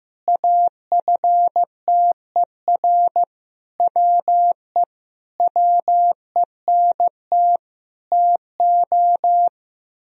Morse code, heard as a keyed tone, about 15 wpm, 700 Hz. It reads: AFTER WE WENT TO
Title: Sentences from Top 1000 Words - Review Part 2 - 15wpm